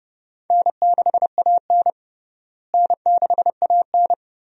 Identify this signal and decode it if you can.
Morse code, heard as a keyed tone, about 32 wpm, 700 Hz. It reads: N6AN N6AN
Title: USA Call Signs - 4 Characters - Repeated 2x - 30wpm